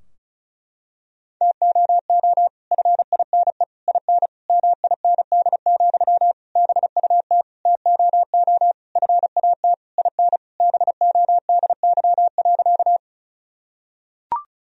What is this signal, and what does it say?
Morse code, heard as a keyed tone, about 36 wpm, 700 Hz. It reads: TOO FINE IN MIND, BUT TOO FAT IN BODY.
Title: Sentences from Top 1000 Words - Review Part 4 - 35wpm